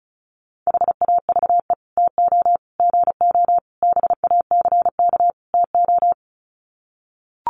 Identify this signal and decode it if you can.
Morse code, heard as a keyed tone, about 34 wpm, 700 Hz. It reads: HAVE TO GO BACK TO
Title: Sets of 5 Words Encore2 35wpm